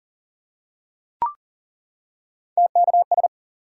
Morse code, heard as a keyed tone, about 40 wpm, 700 Hz. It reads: TKS
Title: Top 90 QSO Elements 40wpm